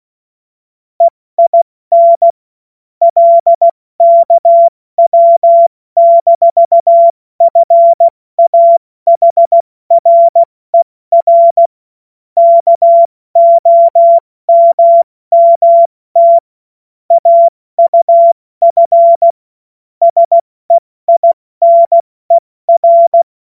Morse code, heard as a keyed tone, about 16 wpm, 680 Hz. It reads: EIN LKW-FAHRER KOMMT AUF SEINER
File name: Morsekurs / Witze / Witz 31. Mai 2022 70 F80